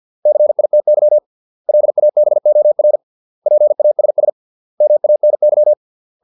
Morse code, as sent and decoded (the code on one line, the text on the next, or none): -.-. .. - -.-- / .-.. .- -... --- .-. / .--. .- ... ... / -- .- -. -.--
CITY LABOR PASS MANY